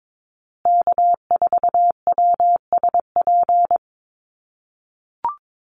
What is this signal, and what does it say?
Morse code, heard as a keyed tone, about 22 wpm, 700 Hz. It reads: K4WSP
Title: USA Call Signs Repeated 2x 22wpm